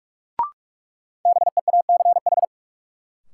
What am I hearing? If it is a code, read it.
Morse code, heard as a keyed tone, about 45 wpm, 700 Hz. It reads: BEACH
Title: Sentences from Top 1500 Words - New Word Drill Alt - 45wpm